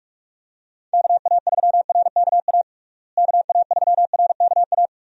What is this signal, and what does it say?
Morse code, heard as a keyed tone, about 46 wpm, 700 Hz. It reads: KA3RKA KA3RKA
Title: USA Call Signs - 6 Characters - Repeated 2x - 45wpm